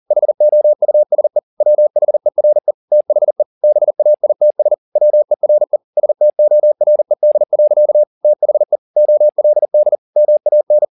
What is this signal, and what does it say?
Morse code, heard as a keyed tone, about 40 wpm, 600 Hz. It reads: HOUSE WHERE THE BAITS WERE STORED. THE OLD MAN